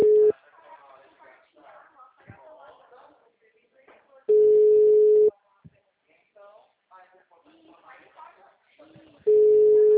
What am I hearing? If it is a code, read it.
Ringback tone (Europe, 425 Hz), with speech.
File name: out- -2016- - - .26016